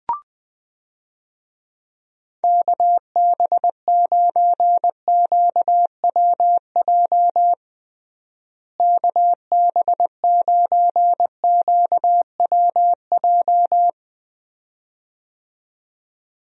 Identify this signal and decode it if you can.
Morse code, heard as a keyed tone, about 20 wpm, 700 Hz. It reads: KB9QWJ KB9QWJ
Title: USA Call Signs - 6 Characters - Repeated 2x - 20wpm